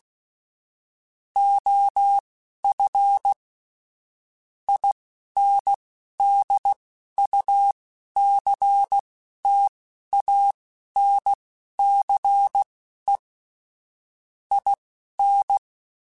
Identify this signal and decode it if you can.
Morse code, heard as a keyed tone, about 16 wpm, spaced as for 11 wpm, 800 Hz. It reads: OF INDUCTANCE IN
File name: Technical Subjects / Technical Subjects-mp3s / XL @10